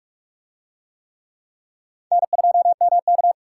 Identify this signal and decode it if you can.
Morse code, heard as a keyed tone, about 46 wpm, 700 Hz. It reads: N2MK